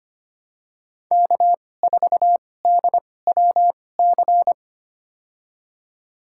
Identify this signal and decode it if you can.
Morse code, heard as a keyed tone, about 25 wpm, spaced as for 18 wpm, 700 Hz. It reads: K4DWC
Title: USA Call Signs - 5 Characters - 2x Character Spacing - 25wpm